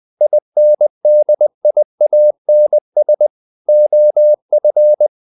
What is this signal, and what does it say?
Morse code, heard as a keyed tone, about 20 wpm, 600 Hz. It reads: INDIANS OF
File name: the old man and the sea / the old man and the sea part 1 20wpm